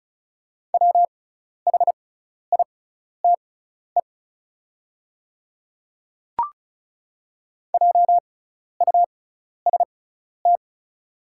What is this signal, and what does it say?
Morse code, heard as a keyed tone, about 34 wpm, spaced as for 12 wpm, 690 Hz. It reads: WHITE JUST
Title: 6x Character Spacing - Top 300 Words - 35wpm